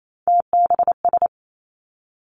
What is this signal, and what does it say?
Morse code, heard as a keyed tone, about 28 wpm, 700 Hz. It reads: TBS
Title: Three Challenging Letter-Number Combinations 28wpm